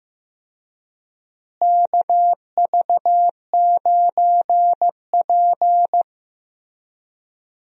Morse code, heard as a keyed tone, about 15 wpm, 700 Hz. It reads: KV9P
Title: USA Call Signs - 4 Characters - Repeated 3x - 15wpm